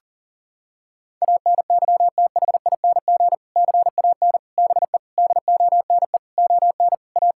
Morse code, heard as a keyed tone, about 40 wpm, 700 Hz. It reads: ANYTHING CAN BE DONE ON A
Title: Sentences from Top 1000 Words - Review Part 5 - 40wpm